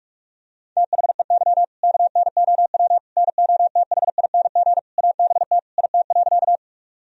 Morse code, heard as a keyed tone, about 46 wpm, 700 Hz. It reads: THEY KNOW NOTHING ABT IT.
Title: Sentences from Top 500 Words and Abbreviations 45wpm